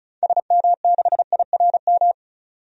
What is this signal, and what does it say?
Morse code, heard as a keyed tone, about 34 wpm, 700 Hz. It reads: SM6IRM
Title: DXCC-100 Call Signs - 6 Characters - 35wpm